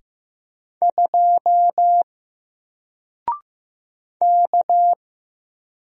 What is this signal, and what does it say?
Morse code, heard as a keyed tone, about 15 wpm, 700 Hz. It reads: 2 K
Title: Single Letter-Number Alt 15wpm